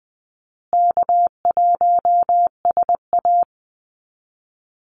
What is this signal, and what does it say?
Morse code, heard as a keyed tone, about 19 wpm, 700 Hz. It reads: K1SA
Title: Easy Call Signs Repeated 3x - 20wpm